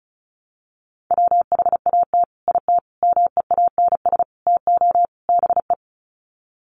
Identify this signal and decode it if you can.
Morse code, heard as a keyed tone, about 34 wpm, 700 Hz. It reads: WHAT IT MEANS TO BE
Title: Sets of 5 Words 35wpm